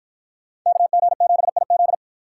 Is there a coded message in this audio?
Yes: Morse code from 0.7 to 1.9 s, heard as a keyed tone, about 53 wpm, 700 Hz. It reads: KG7IZ